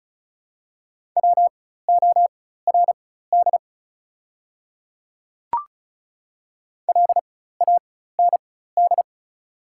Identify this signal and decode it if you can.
Morse code, heard as a keyed tone, about 34 wpm, spaced as for 16 wpm, 700 Hz. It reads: WORD LAND
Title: 4x Character Spacing - Top 200 Words - 35wpm